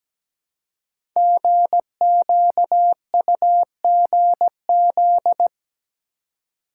Morse code, heard as a keyed tone, about 17 wpm, 700 Hz. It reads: GQUGZ
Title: Code Groups - 5 Letters - 17wpm